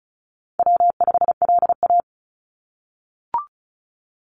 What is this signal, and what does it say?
Morse code, heard as a keyed tone, about 36 wpm, 700 Hz. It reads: W5LA
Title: Call Signs - 35wpm